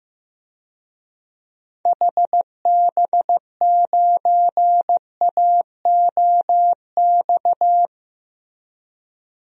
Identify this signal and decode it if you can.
Morse code, heard as a keyed tone, about 16 wpm, 700 Hz. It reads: HB9AOX
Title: DXCC-100 Call Signs - 6 Characters - Repeated 2x - 15wpm